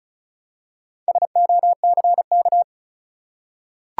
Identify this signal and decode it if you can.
Morse code, heard as a keyed tone, about 36 wpm, 700 Hz. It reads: SOCK
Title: Four Letter Words - 35wpm